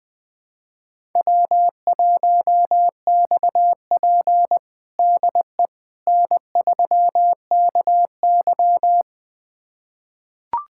Morse code, heard as a keyed tone, about 20 wpm, 700 Hz. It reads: W1XP DE N3KY
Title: USA Callsign DE Callsign - 4 Characters - 20wpm